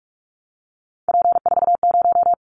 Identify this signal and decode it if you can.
Morse code, heard as a keyed tone, about 46 wpm, 700 Hz. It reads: P40
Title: Callsign Prefixes for Top 100 DXCC - 45wpm